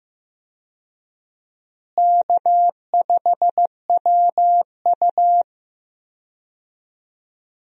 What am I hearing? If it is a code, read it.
Morse code, heard as a keyed tone, about 15 wpm, 700 Hz. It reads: K5WU
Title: USA Call Signs - 4 Characters - Repeated 3x - 15wpm